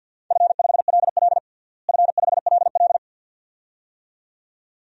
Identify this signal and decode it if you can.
Morse code, heard as a keyed tone, about 46 wpm, 700 Hz. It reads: F5LL F5LL
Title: DXCC-100 Call Signs - 4 Characters - Repeated 2x - 50wpm